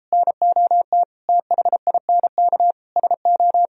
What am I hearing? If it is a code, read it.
Morse code, heard as a keyed tone, about 34 wpm, 700 Hz. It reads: NOT THINK SO